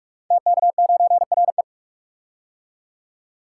Morse code, heard as a keyed tone, about 44 wpm, 700 Hz. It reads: TK9RE
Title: Code Groups - 5 Letter:Numbers - 45wpm